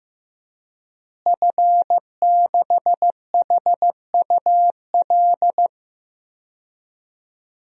Morse code, heard as a keyed tone, about 15 wpm, 700 Hz. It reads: F6HUL
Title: DXCC-100 Call Signs - 5 Characters - Repeated 2x - 15wpm